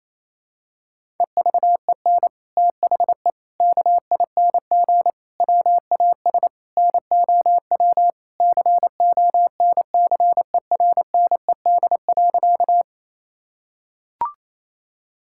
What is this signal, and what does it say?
Morse code, heard as a keyed tone, about 27 wpm, 700 Hz. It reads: EVEN THE KING WAS NOW CONCERNED.
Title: Sentences from Top 1000 Words - Review Part 6 - 28wpm